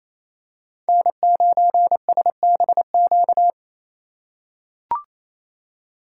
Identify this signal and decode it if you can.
Morse code, heard as a keyed tone, about 28 wpm, 700 Hz. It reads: N9SBQ
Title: USA Call Signs - 5 Characters - 4x Character Spacing - 28wpm